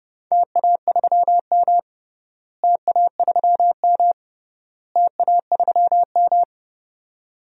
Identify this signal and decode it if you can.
Morse code, heard as a keyed tone, about 32 wpm, 700 Hz. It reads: TA3M TA3M TA3M
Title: DXCC-100 Call Signs - 4 Characters - Repeated 3x - 30wpm